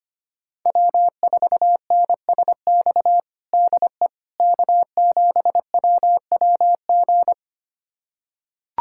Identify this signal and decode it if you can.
Morse code, heard as a keyed tone, about 25 wpm, 700 Hz. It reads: W4NSX DE K7WWG
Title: USA Callsign DE Callsign 25wpm